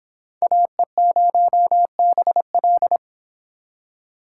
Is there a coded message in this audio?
Yes: Morse code from 0.4 to 3.0 s, heard as a keyed tone, about 26 wpm, 700 Hz. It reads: AE0BL